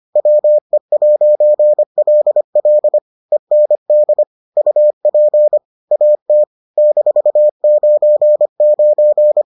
Morse code, heard as a keyed tone, about 25 wpm, 600 Hz. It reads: WE'LL END UP AT -99